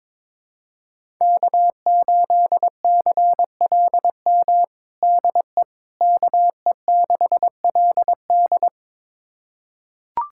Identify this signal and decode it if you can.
Morse code, heard as a keyed tone, about 22 wpm, 700 Hz. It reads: K8CLM DE KE6LD
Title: USA Callsign DE Callsign - 5 Characters - 22wpm